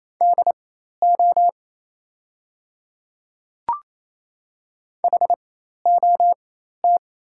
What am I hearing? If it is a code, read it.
Morse code, heard as a keyed tone, about 28 wpm, spaced as for 13 wpm, 700 Hz. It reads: DO HOT
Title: 4x Character Spacing - Top 100 Words - 28wpm